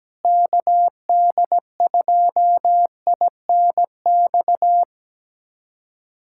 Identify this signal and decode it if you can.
Morse code, heard as a keyed tone, about 17 wpm, 700 Hz. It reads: KD2INX
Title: USA Callsign DE Callsign 17wpm